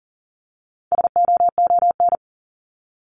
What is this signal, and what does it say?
Morse code, heard as a keyed tone, about 40 wpm, 700 Hz. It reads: SOON
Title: Top 500 Words 40wpm